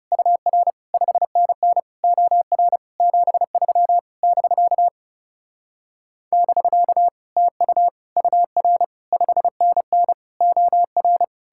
Morse code, heard as a keyed tone, about 38 wpm, 700 Hz. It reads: UR 5NN OR 73 <BK> <BK> TU UR 5NN OR